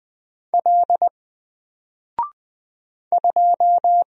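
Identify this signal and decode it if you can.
Morse code, heard as a keyed tone, about 20 wpm, 700 Hz. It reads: L 2
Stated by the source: Single letter and number Morse drill at 20 wpm, from N4LCD.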